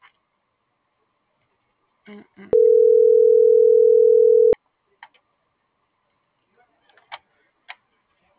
Ringback tone (North America), 440 + 480 Hz.
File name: CENTRAL GENERAL / out karla.sky - .11658